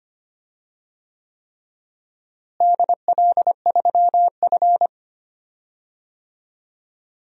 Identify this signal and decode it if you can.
Morse code, heard as a keyed tone, about 25 wpm, 700 Hz. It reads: DL3F
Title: DXCC-100 Call Signs - 4 Characters - Repeated 3x - 25wpm